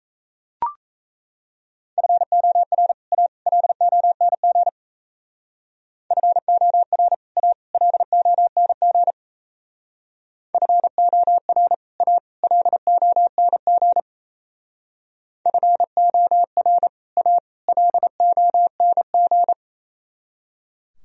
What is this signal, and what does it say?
Morse code, heard as a keyed tone, about 33 wpm, 700 Hz. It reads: FOR A LONG FOR A LONG FOR A LONG FOR A LONG
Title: Speed-Racing - Sets of 3 Words 28wpm